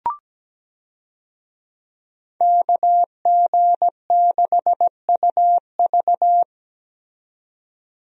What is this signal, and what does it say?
Morse code, heard as a keyed tone, about 17 wpm, 700 Hz. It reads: KG6UV
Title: USA Call Signs - 5 Characters - 17wpm